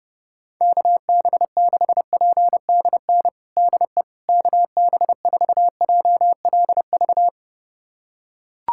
Morse code, heard as a keyed tone, about 30 wpm, 700 Hz. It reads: KB6PDN DE KB4JLV
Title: USA Callsign DE Callsign - 6 Characters - 30wpm